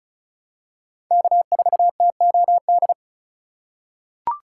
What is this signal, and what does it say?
Morse code, heard as a keyed tone, about 36 wpm, 700 Hz. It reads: K4TOD
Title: USA Call Signs - 5 Characters - 6x Character Spacing - 35wpm